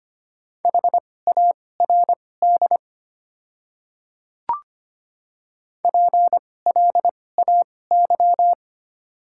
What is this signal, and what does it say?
Morse code, heard as a keyed tone, about 25 wpm, spaced as for 18 wpm, 700 Hz. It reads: HARD PLAY
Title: 2x Character Spacing - Top 300 Words - 25wpm